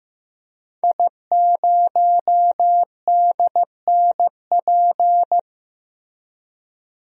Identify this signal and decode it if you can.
Morse code, heard as a keyed tone, about 15 wpm, 700 Hz. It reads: I0DNP
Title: DXCC-100 Call Signs - 5 Characters - Repeated 3x - 15wpm